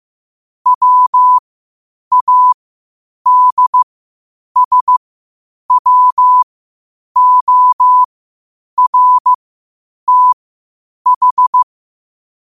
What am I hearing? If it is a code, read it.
Morse code, heard as a keyed tone, about 15 wpm, spaced as for 8 wpm, 1000 Hz. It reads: WADSWORTH
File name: qotdfiles / 8wpm